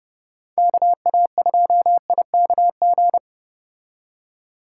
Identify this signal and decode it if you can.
Morse code, heard as a keyed tone, about 29 wpm, 700 Hz. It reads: KA2IKG